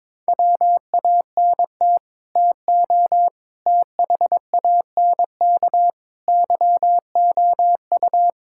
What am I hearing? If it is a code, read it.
Morse code, heard as a keyed tone, about 22 wpm, 700 Hz. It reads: WANT TO THANK YOU
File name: Sets of 9 Words - 22wpm